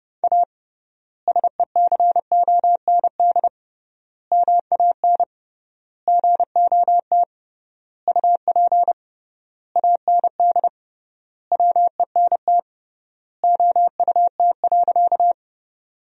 Morse code, heard as a keyed tone, about 30 wpm, 700 Hz. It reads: A SECOND MAN GOT UP AND WENT OUT.
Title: 3x Word-Spacing - Sentences from Top 300 Words - 30wpm